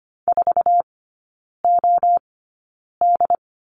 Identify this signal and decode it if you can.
Morse code, heard as a keyed tone, about 25 wpm, spaced as for 9 wpm, 690 Hz. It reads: 4OD
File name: Lesson 15e - Number 5 - Three Characters 25wpm